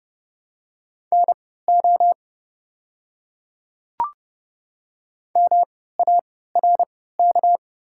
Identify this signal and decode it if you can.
Morse code, heard as a keyed tone, about 30 wpm, spaced as for 17 wpm, 700 Hz. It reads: NO MARK
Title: 3x Character Spacing - Top 300 Words - 30wpm